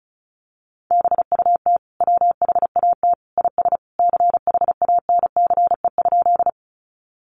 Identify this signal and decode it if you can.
Morse code, heard as a keyed tone, about 34 wpm, 700 Hz. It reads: BUT WHAT IS CHANCE?
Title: Sentences from Top 800 Words 35wpm